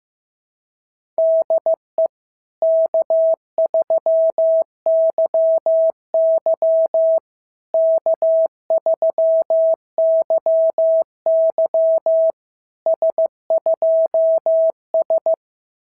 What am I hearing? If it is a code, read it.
Morse code, heard as a keyed tone, about 15 wpm, 650 Hz. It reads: DE K3YY K3YY S2S
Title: SOTA QSO Practice 15wpm